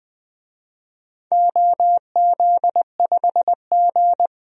Morse code, heard as a keyed tone, about 20 wpm, 700 Hz. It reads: OZ5G
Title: DXCC-100 Call Signs - 4 Characters - Repeated 3x - 20wpm